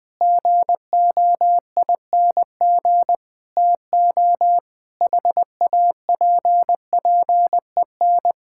Morse code, heard as a keyed tone, about 21 wpm, 700 Hz. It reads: GOING TO HAPPEN
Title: Sentences with 7-Word Phrases - 20wpm